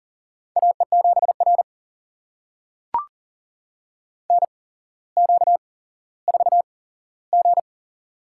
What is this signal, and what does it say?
Morse code, heard as a keyed tone, about 41 wpm, spaced as for 17 wpm, 700 Hz. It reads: AE7R NQ4G
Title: USA Call Signs - 4 Characters - 8x Character Spacing - 40wpm